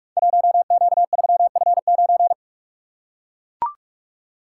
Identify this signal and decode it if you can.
Morse code, heard as a keyed tone, about 44 wpm, 690 Hz. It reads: 1Q3F9
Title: Code Groups - 5 Letter:Numbers - 45wpm